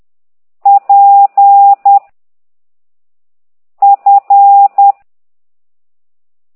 Morse code, heard as a keyed tone, about 10 wpm, 800 Hz. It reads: P F
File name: ct2idw / exerc8